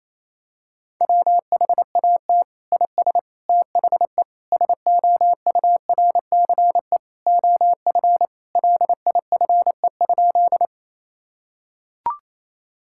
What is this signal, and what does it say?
Morse code, heard as a keyed tone, about 28 wpm, 700 Hz. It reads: WHAT IS THE SOURCE OF LIFE?
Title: Sentences from Top 1000 Words - Review Part 5 - 28wpm